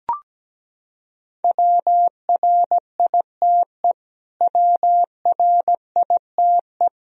Morse code, heard as a keyed tone, about 17 wpm, 700 Hz. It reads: WRITE WRITE
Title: Top 100 Words Repeated 17wpm